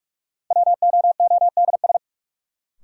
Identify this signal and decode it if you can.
Morse code, heard as a keyed tone, about 45 wpm, 700 Hz. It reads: WOODS
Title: Sentences from Top 1000 Words - Review Part 2 - New Word Drill - 45wpm